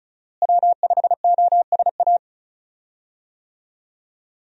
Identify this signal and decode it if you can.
Morse code, heard as a keyed tone, about 34 wpm, 700 Hz. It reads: W5OSA
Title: USA Call Signs - 5 Characters - Repeated 2x - 35wpm